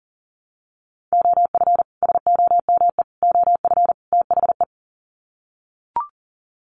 Morse code, heard as a keyed tone, about 40 wpm, 700 Hz. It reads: OF SOME OF THE
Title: Sets of 4 Words 40wpm